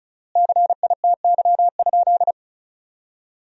Morse code, heard as a keyed tone, about 36 wpm, 700 Hz. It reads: CITY?
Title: 4x Word-Spacing - Sentences from Top 300 Words - 35wpm